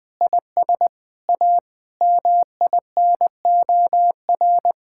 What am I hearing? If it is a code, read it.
Morse code, heard as a keyed tone, about 21 wpm, 700 Hz. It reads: IS A MINOR
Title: Sentences with 6-Word Phrases - 20wpm